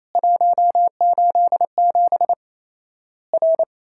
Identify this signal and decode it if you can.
Morse code, heard as a keyed tone, about 28 wpm, 700 Hz. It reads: 187 R
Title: Worked All Europe DX Contest - QTC Exchange - 28wpm